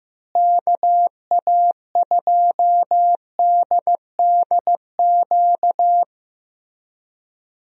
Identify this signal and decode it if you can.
Morse code, heard as a keyed tone, about 15 wpm, 700 Hz. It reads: KA2DDQ
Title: USA Call Signs - 6 Characters - Repeated 3x - 15wpm